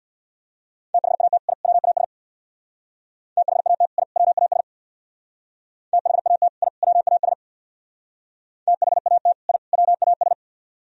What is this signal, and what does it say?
Morse code, heard as a keyed tone, about 64 wpm, 700 Hz. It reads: THAT I WAS THAT I WAS THAT I WAS THAT I WAS